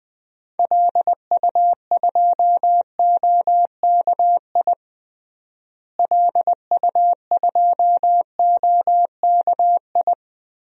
Morse code, heard as a keyed tone, about 21 wpm, 700 Hz. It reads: LU2OKI LU2OKI
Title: DXCC-100 Call Signs - 6 Characters - Repeated 3x - 20wpm